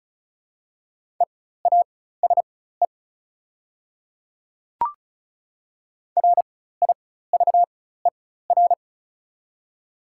Morse code, heard as a keyed tone, about 34 wpm, spaced as for 16 wpm, 690 Hz. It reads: EASE RIVER
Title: 4x Character Spacing - Top 300 Words - 35wpm